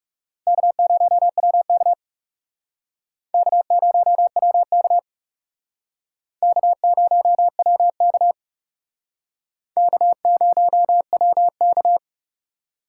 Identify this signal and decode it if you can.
Morse code, heard as a keyed tone, about 38 wpm, 700 Hz. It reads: K0WK K0WK K0WK K0WK